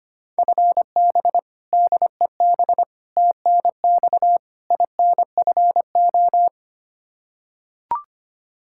Morse code, heard as a keyed tone, about 25 wpm, 700 Hz. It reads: FB DEB TNX INFO
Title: K1USN SST QSO Practice 25wpm